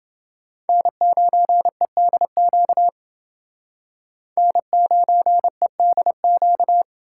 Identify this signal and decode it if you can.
Morse code, heard as a keyed tone, about 29 wpm, 700 Hz. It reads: N9EDQ N9EDQ